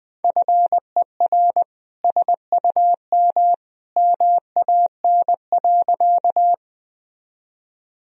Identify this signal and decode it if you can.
Morse code, heard as a keyed tone, about 20 wpm, 700 Hz. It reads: FER SUM MAN.
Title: Sentences from Top 200 Words and Abbreviations 20wpm